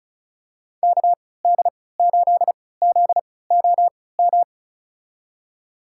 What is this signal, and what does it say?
Morse code, heard as a keyed tone, about 36 wpm, spaced as for 20 wpm, 700 Hz. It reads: KD8ZOM
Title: USA Call Signs - 6 Characters - 3x Character Spacing - 35wpm